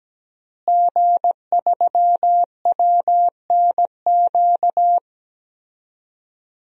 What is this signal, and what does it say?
Morse code, heard as a keyed tone, about 17 wpm, 700 Hz. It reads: G3WNQ
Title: DXCC-100 Call Signs - 5 Characters - Repeated 3x - 17wpm